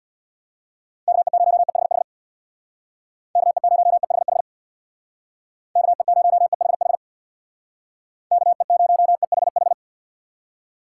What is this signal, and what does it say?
Morse code, heard as a keyed tone, about 64 wpm, 700 Hz. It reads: KE0EHH KE0EHH KE0EHH KE0EHH